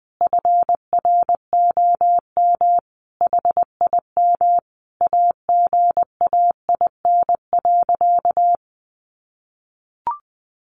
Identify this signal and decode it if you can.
Morse code, heard as a keyed tone, about 20 wpm, 700 Hz. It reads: FROM HIM AGAIN.
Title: Sentences from Top 1300 Words 20wpm